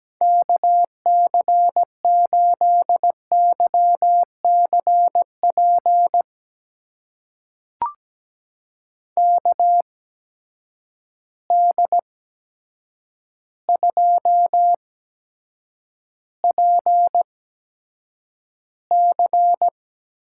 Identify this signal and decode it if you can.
Morse code, heard as a keyed tone, about 17 wpm, spaced as for 8 wpm, 700 Hz. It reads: KC8YCP KD2PC
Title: USA Call Signs - 6 Characters - 8x Character Spacing - 17wpm